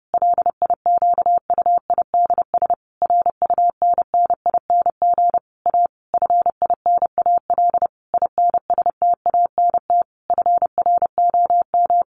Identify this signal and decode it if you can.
Morse code, heard as a keyed tone, about 30 wpm, 700 Hz. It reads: LIQUIDS RUNNING A FINAL INSTANT FROM
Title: A Sound Of Thunder 30wpm